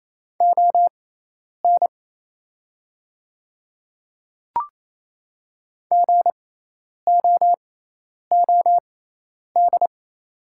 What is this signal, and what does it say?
Morse code, heard as a keyed tone, about 28 wpm, spaced as for 10 wpm, 700 Hz. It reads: ON GOOD
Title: 6x Character Spacing - Top 200 Words - 28wpm